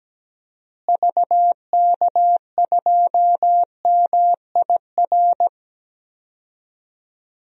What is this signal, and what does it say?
Morse code, heard as a keyed tone, about 17 wpm, 700 Hz. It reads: VK2MIR
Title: DXCC-100 Call Signs - 6 Characters - Repeated 2x - 17wpm